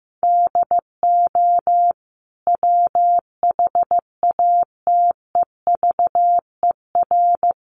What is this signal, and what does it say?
Morse code, heard as a keyed tone, about 15 wpm, 700 Hz. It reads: DO WHATEVER
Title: Sentences from Top 1100 Words 15wpm